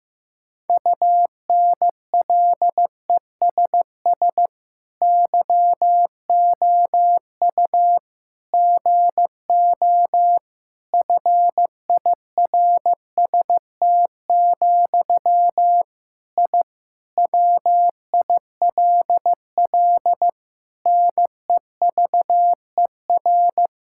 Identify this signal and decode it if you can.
Morse code, heard as a keyed tone, about 15 wpm, 700 Hz. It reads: UNLESS YOU GO FIRST, I WILL NEVER
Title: Sentences from Top 1000 Words - Review Part 6 - 15wpm